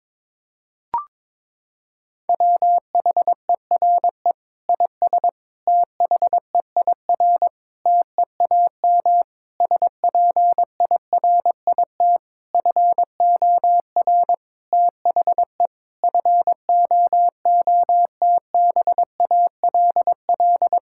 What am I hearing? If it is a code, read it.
Morse code, heard as a keyed tone, about 22 wpm, 700 Hz. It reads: WHERE IS THEIR TEAM SPIRIT FOR THE FOOTBALL
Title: Sentences from Top 1600 Words 22wpm